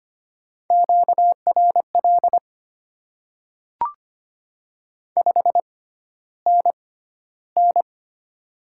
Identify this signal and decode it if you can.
Morse code, heard as a keyed tone, about 25 wpm, spaced as for 13 wpm, 700 Hz. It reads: QRL 5NN
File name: Top 20 QSO Elements - 6x Character Spacing - 25wpm